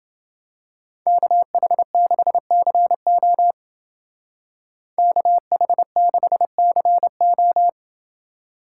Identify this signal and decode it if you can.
Morse code, heard as a keyed tone, about 29 wpm, 700 Hz. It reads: KH6CO KH6CO